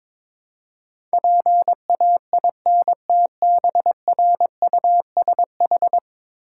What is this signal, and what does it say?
Morse code, heard as a keyed tone, about 22 wpm, 700 Hz. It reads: PAINTBRUSH
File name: Difficult Words with Hints - 22wpm